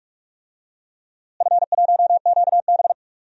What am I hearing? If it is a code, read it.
Morse code, heard as a keyed tone, about 46 wpm, 700 Hz. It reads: F1QB
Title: DXCC-100 Call Signs - 4 Characters - Repeated 2x - 45wpm